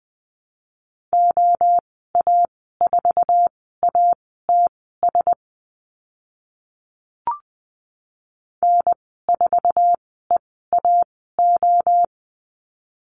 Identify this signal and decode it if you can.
Morse code, heard as a keyed tone, about 20 wpm, spaced as for 14 wpm, 700 Hz. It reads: OA4ATS N4EAO
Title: 088 Lesson 15g - Number 5 - Call Signs 20wpm